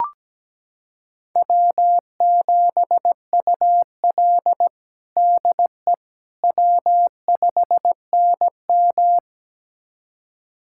Morse code, heard as a keyed tone, about 17 wpm, 700 Hz. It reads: W7UL DE W5NM
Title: USA Callsign DE Callsign - 4 Characters - 17wpm